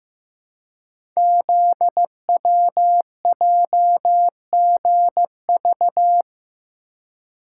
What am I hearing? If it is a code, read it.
Morse code, heard as a keyed tone, about 15 wpm, 700 Hz. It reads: ZWJGV
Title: Code Groups - 5 Letters - 15wpm